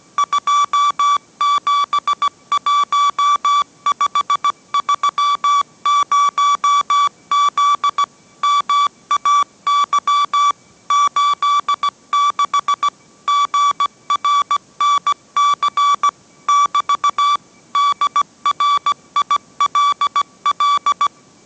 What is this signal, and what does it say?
Morse code, heard as a keyed tone, about 17 wpm, 1150 Hz. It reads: 271530Z MAY 86 GRNC <BT> DRILL